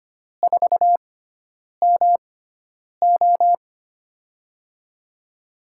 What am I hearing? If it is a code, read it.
Morse code, heard as a keyed tone, about 25 wpm, spaced as for 9 wpm, 700 Hz. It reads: 4MO
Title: USA Call Signs - 5 Characters - 6x Character Spacing - 25wpm